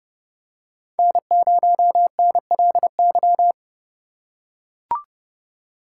Morse code, heard as a keyed tone, about 31 wpm, 700 Hz. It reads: N0NLY